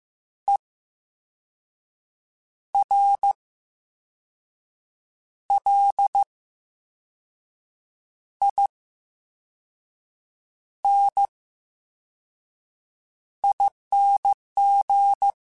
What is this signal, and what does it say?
Morse code, heard as a keyed tone, about 15 wpm, spaced as for 5 wpm, 800 Hz. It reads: ERLINING
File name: Special INGs / Special INGs-mp3s / Slow ing @13